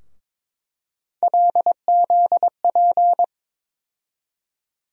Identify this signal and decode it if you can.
Morse code, heard as a keyed tone, about 21 wpm, 700 Hz. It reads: LZP
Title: USA Call Sign Postfixes - 3 Letters 22wpm